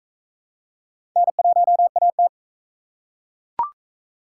Morse code, heard as a keyed tone, about 40 wpm, 700 Hz. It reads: N1AT